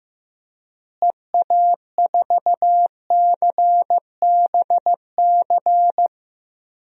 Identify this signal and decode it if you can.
Morse code, heard as a keyed tone, about 16 wpm, 700 Hz. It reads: EA4CBC
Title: DXCC-100 Call Signs - 6 Characters - Repeated 2x - 15wpm